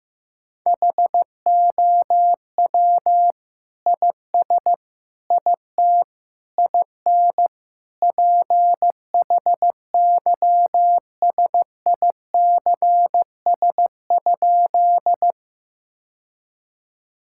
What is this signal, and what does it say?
Morse code, heard as a keyed tone, about 16 wpm, 700 Hz. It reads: HOW IS IT IN PHYSICS?
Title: Sentences from Top 1000 Words - Review Part 5 - 15wpm